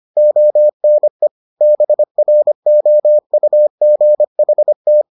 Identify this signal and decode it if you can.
Morse code, heard as a keyed tone, about 25 wpm, 600 Hz. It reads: ONE BROUGHT